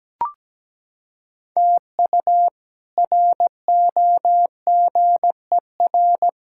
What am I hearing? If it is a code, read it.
Morse code, heard as a keyed tone, about 17 wpm, 700 Hz. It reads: TU ROGER
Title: K1USN SST QSO Practice - 17wpm